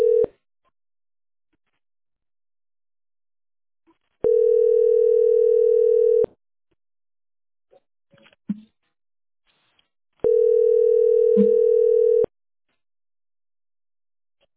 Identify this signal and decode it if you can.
Ringback tone (North America), 440 + 480 Hz.
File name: fabc509b-a977-4383-a51d